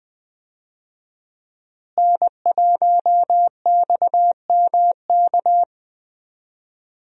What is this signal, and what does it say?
Morse code, heard as a keyed tone, about 20 wpm, 700 Hz. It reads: N1XMK
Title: USA Call Signs - 5 Characters - Repeated 2x - 20wpm